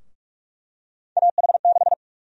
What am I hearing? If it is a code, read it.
Morse code, heard as a keyed tone, about 46 wpm, 700 Hz. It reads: AH6
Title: USA Call Sign Prefixes - Repeated 2x - 45wpm